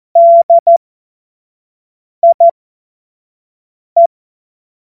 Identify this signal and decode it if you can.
Morse code, heard as a keyed tone, about 14 wpm, spaced as for 5 wpm, 680 Hz. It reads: DIE
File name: Morsekurs / Witze / Witz 04. Oktober 2022 25 F70